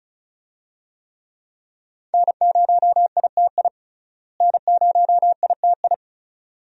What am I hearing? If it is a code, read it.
Morse code, heard as a keyed tone, about 35 wpm, 700 Hz. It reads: N0ITI N0ITI
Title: USA Call Signs - 5 Characters - Repeated 2x - 35wpm